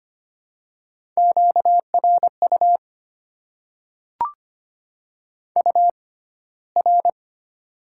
Morse code, heard as a keyed tone, about 25 wpm, spaced as for 15 wpm, 700 Hz. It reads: QRU UR
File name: Top 90 QSO Elements - 6x Character Spacing - 25wpm